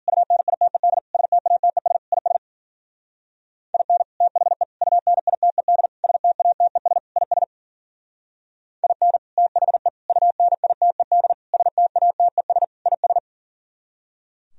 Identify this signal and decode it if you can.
Morse code, heard as a keyed tone, about 51 wpm, 700 Hz. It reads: UNITED STATES IS IN THE UNITED STATES IS IN THE UNITED STATES IS